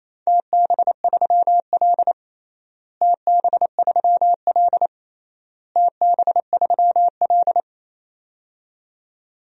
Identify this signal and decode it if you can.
Morse code, heard as a keyed tone, about 28 wpm, 700 Hz. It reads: TB3L TB3L TB3L
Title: DXCC-100 Call Signs - 4 Characters - Repeated 3x - 28wpm